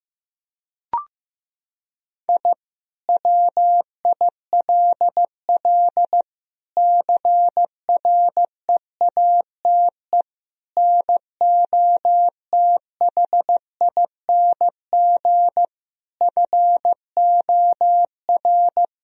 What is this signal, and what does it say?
Morse code, heard as a keyed tone, about 15 wpm, 700 Hz. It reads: I WILL CREATE NOTHING FOR
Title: Sentences from Top 1000 Words - Review Part 3 - 15wpm